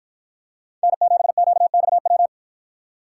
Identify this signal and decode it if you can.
Morse code, heard as a keyed tone, about 50 wpm, 700 Hz. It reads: N7QXW